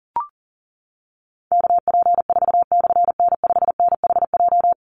Morse code, heard as a keyed tone, about 40 wpm, 700 Hz. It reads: KP4/N5NHJ
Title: Call Signs Repeated 3x - 40wpm